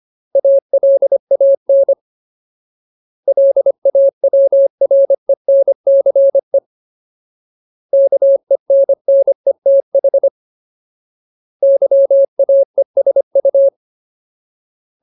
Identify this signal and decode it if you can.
Morse code, heard as a keyed tone, about 25 wpm, 550 Hz. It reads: ALAN LAWRENCE KENNETH YAESU
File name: Ultimate Word List / 25WPM Ultimate